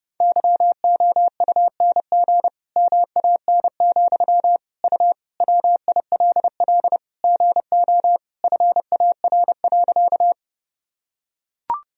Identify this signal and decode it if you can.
Morse code, heard as a keyed tone, about 30 wpm, 700 Hz. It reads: YOUNG MAN, U WILL GO FAR.
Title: Sentences from Top 400 Words and Abbreviations 30wpm